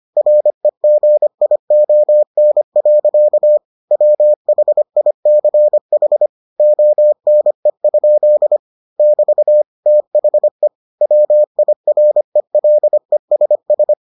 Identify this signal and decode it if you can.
Morse code, heard as a keyed tone, about 25 wpm, 600 Hz. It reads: REGION. WHICH ONE? <BT> THE WIRELESS